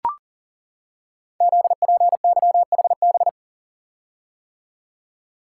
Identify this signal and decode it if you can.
Morse code, heard as a keyed tone, about 41 wpm, 690 Hz. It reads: ZPYHB
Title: Code Groups - 5 Letter:Numbers - 40wpm